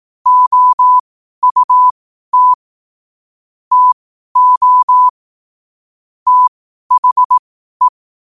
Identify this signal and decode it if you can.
Morse code, heard as a keyed tone, about 18 wpm, spaced as for 12 wpm, 1000 Hz. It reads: OUT TO THE
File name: qotdfiles / 12wpm